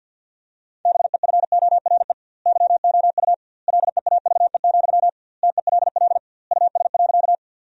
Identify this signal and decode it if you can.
Morse code, heard as a keyed tone, about 50 wpm, 700 Hz. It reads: BEFORE YOU LEAVE, TELL US.
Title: 2x Word-Spacing - Sentences from Top 400 Words - 50wpm